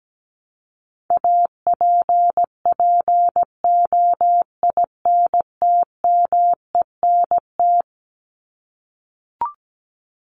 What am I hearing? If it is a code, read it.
Morse code, heard as a keyed tone, about 17 wpm, 700 Hz. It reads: APPOINTMENT
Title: Top 1751-2000 Words 17wpm